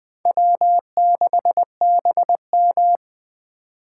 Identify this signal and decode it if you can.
Morse code, heard as a keyed tone, about 20 wpm, 700 Hz. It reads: W6BM
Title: Speed Racing - USA 1x2 Call Signs 17wpm